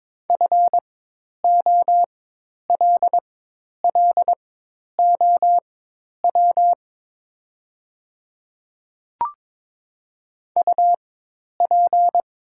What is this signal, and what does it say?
Morse code, heard as a keyed tone, about 22 wpm, spaced as for 10 wpm, 700 Hz. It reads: FOLLOW UP
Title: 4x Character Spacing - Top 200 Words - 22wpm